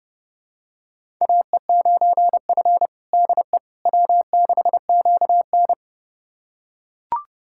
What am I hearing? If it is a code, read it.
Morse code, heard as a keyed tone, about 30 wpm, 700 Hz. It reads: AE9F DE W6QN
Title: USA Callsign DE Callsign - 4 Characters - 30wpm